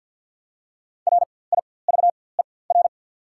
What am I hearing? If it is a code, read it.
Morse code, heard as a keyed tone, about 48 wpm, spaced as for 23 wpm, 700 Hz. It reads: RIVER
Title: 4x Character Spacing - Top 300 Words - 50wpm